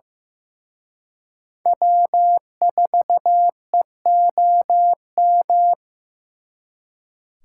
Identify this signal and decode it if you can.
Morse code, heard as a keyed tone, about 18 wpm, 700 Hz. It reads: W4EOM
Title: Speed Racing - USA 1x3 Call Signs 15wpm